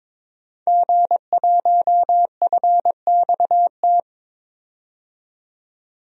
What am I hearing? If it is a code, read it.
Morse code, heard as a keyed tone, about 22 wpm, 700 Hz. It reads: G1FXT
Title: DXCC-100 Call Signs - 5 Characters - 22wpm